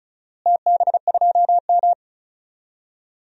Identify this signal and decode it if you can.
Morse code, heard as a keyed tone, about 34 wpm, 700 Hz. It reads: TB2M
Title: DXCC-100 Call Signs - 4 Characters - Repeated 2x - 35wpm